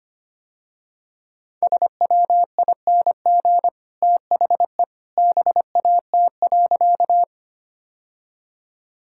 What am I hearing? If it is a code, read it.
Morse code, heard as a keyed tone, about 25 wpm, 700 Hz. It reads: SWING THE BAT.
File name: Easy Sentences with Hints - 25wpm